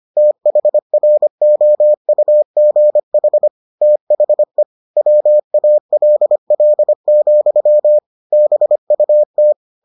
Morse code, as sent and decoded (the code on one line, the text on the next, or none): - .... .-. --- ..- --. .... / - .... / .-- .- .-.. .-.. --..-- / -... ..- -
THROUGH THE WALL, BUT